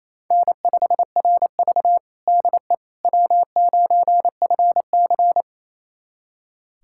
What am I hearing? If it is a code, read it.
Morse code, heard as a keyed tone, about 28 wpm, 700 Hz. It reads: N5RV DE W9FC
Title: USA Callsign DE Callsign - 4 Characters - 28wpm